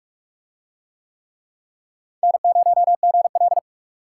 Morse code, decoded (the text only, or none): N0GL